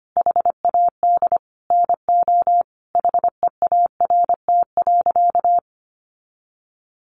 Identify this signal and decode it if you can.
Morse code, heard as a keyed tone, about 25 wpm, 700 Hz. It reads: HAD NO HEART.
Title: Sentences from Top 1200 Words 25wpm